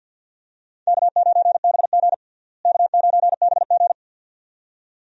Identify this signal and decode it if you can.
Morse code, heard as a keyed tone, about 47 wpm, 700 Hz. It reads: K9BG K9BG
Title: USA Call Signs - 4 Characters - Repeated 2x - 50wpm